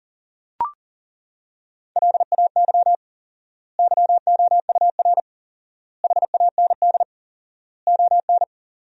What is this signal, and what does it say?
Morse code, heard as a keyed tone, about 40 wpm, 700 Hz. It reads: LAY YOUR HAND ON
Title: 4x Word-Spacing - Sentences from Top 400 Words - 40wpm